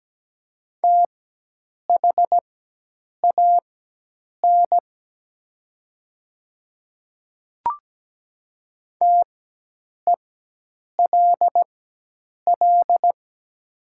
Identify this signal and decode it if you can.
Morse code, heard as a keyed tone, about 17 wpm, spaced as for 8 wpm, 700 Hz. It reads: THAN TELL
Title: 4x Character Spacing - Top 200 Words - 17wpm